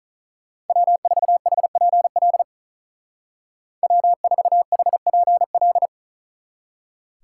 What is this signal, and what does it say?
Morse code, heard as a keyed tone, about 39 wpm, 700 Hz. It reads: W4HPL W4HPL
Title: Speed Racing - USA 1x3 Call Signs 35wpm